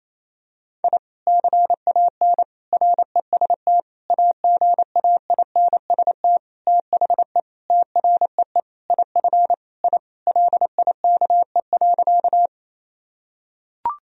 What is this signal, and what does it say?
Morse code, heard as a keyed tone, about 28 wpm, 700 Hz. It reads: I CAN REST AGAINST THE TREE IF I LIKE.
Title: Sentences from Top 1000 Words - Review Part 1 - 28wpm